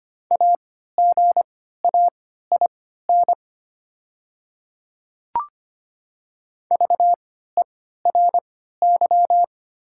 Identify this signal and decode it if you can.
Morse code, heard as a keyed tone, about 25 wpm, spaced as for 14 wpm, 700 Hz. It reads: AGAIN VERY
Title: 3x Character Spacing - Top 300 Words - 25wpm